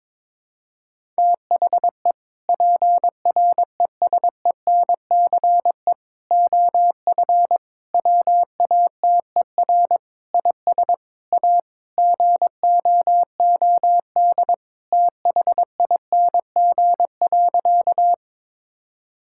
Morse code, heard as a keyed tone, about 21 wpm, 700 Hz. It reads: THE PRESENCE OF WATER IS A GOOD THING.
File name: practice / 22wpm / Sentences with 3-Word Phrases - 22wpm